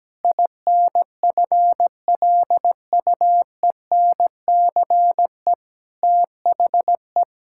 Morse code, heard as a keyed tone, about 17 wpm, 700 Hz. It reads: INFLUENCE THE
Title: Sentences from Top 1600 Words 17wpm